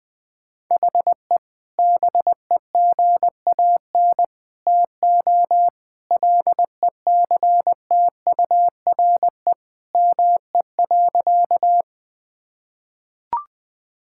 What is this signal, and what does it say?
Morse code, heard as a keyed tone, about 20 wpm, 700 Hz. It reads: HE BEGAN TO LECTURE ME.
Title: Sentences from Top 1600 Words 20wpm